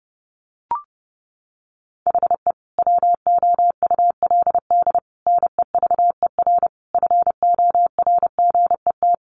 Morse code, heard as a keyed tone, about 30 wpm, 700 Hz. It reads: HE WOULD NEVER FORGET
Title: Sentences from Top 1200 Words 30wpm